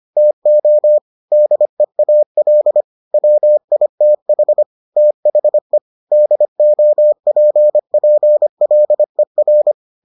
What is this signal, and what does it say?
Morse code, heard as a keyed tone, about 25 wpm, 600 Hz. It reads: TO DEAL WITH THE DOPPLER